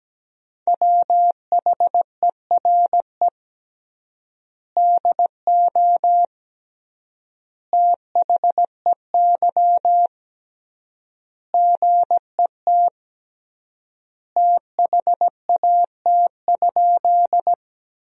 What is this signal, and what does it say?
Morse code, heard as a keyed tone, about 17 wpm, 700 Hz. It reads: WHERE DO THEY GET THAT?
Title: 3x Word-Spacing - Sentences from Top 200 Words - 17wpm